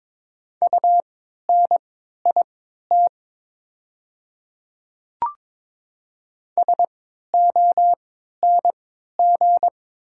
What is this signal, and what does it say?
Morse code, heard as a keyed tone, about 21 wpm, spaced as for 12 wpm, 700 Hz. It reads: UNIT SONG
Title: 3x Character Spacing - Top 500 Words - 22wpm